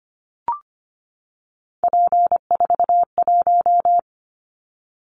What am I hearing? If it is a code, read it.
Morse code, heard as a keyed tone, about 25 wpm, 700 Hz. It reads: P41
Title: Callsign Prefixes for Top 100 DXCC - 25wpm